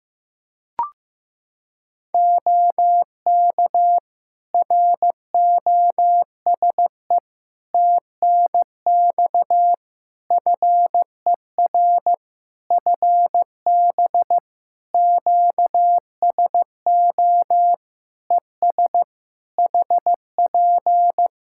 Morse code, heard as a keyed tone, about 15 wpm, 700 Hz. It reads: OK ROSE TNX FER FB QSO ES HP
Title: Practice QSO - Exchange 4 - Ending - 15wpm